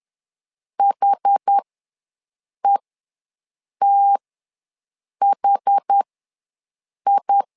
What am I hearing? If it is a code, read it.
Morse code, heard as a keyed tone, about 10 wpm, spaced as for 6 wpm, 800 Hz. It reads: HETHI
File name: LEZIONI / 2 UHDN / 2 UH ws 10wpm 2029